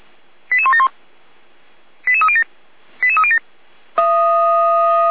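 Five-tone selective call (ZVEI), heard as quick runs of five tones, tones of about 70 ms. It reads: 88171 88307 88307
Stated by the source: ZVEI five-tone selective call 88352 received on an SDR.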